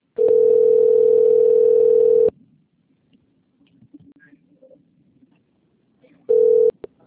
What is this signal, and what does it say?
Ringback tone (North America), 440 + 480 Hz.
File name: rg-1600-unknown- - - .26358